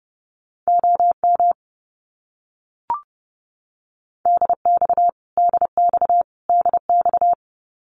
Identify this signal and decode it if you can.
Morse code, heard as a keyed tone, about 30 wpm, 700 Hz. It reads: OM DX DX DX
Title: Abbreviations Repeated 3x - 30wpm